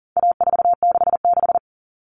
Morse code, heard as a keyed tone, about 37 wpm, 700 Hz. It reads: A466
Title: Common Cut Numbers - 4-Digit 40wpm